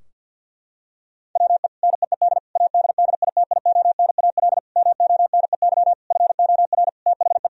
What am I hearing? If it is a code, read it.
Morse code, heard as a keyed tone, about 50 wpm, 700 Hz. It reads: WE NEED ADDITIONAL MONEY FOR THE
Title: Sentences from Top 1000 Words - Review Part 4 - 50wpm